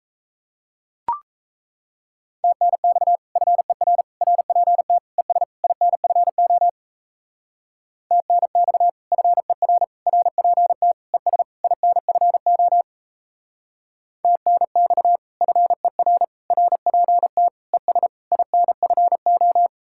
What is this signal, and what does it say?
Morse code, heard as a keyed tone, about 35 wpm, 700 Hz. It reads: TNX FER RPT ES INFO TNX FER RPT ES INFO TNX FER RPT ES INFO
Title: Speed-Racing Common QSO Phrases - 28wpm